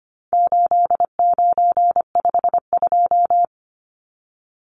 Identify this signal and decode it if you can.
Morse code, heard as a keyed tone, about 25 wpm, 700 Hz. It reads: 8952
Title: Numbers - 4-Digit 25wpm